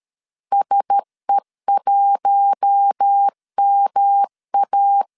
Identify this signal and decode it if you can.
Morse code, heard as a keyed tone, about 13 wpm, 800 Hz. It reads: SE1MA
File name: LEZIONI / 6 PKX0 / 6 PK 12wpm 2006